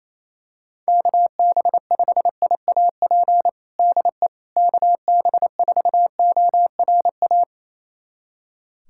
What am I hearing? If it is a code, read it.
Morse code, heard as a keyed tone, about 28 wpm, 700 Hz. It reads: KB5IAP DE KB4ORA
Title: USA Callsign DE Callsign - 6 Characters - 28wpm